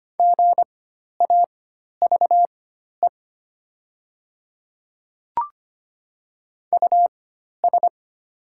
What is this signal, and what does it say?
Morse code, heard as a keyed tone, about 25 wpm, spaced as for 12 wpm, 700 Hz. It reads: GAVE US
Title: 4x Character Spacing - Top 500 Words - 25wpm